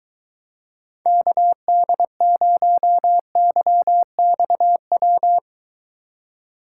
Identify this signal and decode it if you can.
Morse code, heard as a keyed tone, about 22 wpm, 700 Hz. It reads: KD0YXW